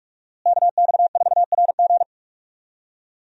Morse code, heard as a keyed tone, about 41 wpm, 700 Hz. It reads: KX4RG
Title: USA Call Signs - 5 Characters - Repeated 2x - 45wpm